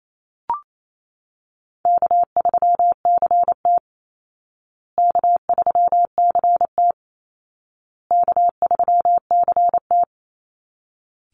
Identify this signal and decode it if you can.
Morse code, heard as a keyed tone, about 28 wpm, 700 Hz. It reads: K3CT K3CT K3CT
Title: Call Signs Repeated 2x - 28wpm